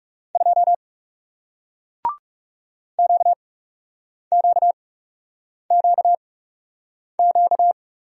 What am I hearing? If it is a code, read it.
Morse code, heard as a keyed tone, about 40 wpm, spaced as for 8 wpm, 700 Hz. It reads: 2 QQQQ